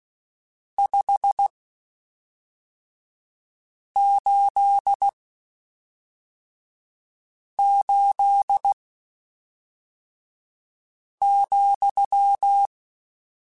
Morse code, heard as a keyed tone, about 16 wpm, spaced as for 3 wpm, 800 Hz. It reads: 588,
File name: (spaced) / QSOs Spaced-mp3s / WA7QOX 1 @5 sp